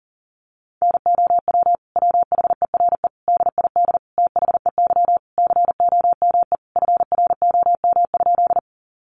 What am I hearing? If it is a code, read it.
Morse code, heard as a keyed tone, about 40 wpm, 700 Hz. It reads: NOW WHERE DID THEY COME FROM?
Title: Sentences from Top 200 Words 40wpm